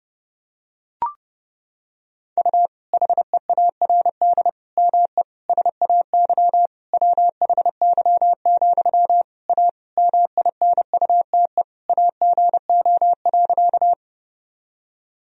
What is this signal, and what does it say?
Morse code, heard as a keyed tone, about 30 wpm, 700 Hz. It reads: U HEARD ME SAY WHY, A MINUTE AGO.
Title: Sentences from Top 500 Words and Abbreviations 30wpm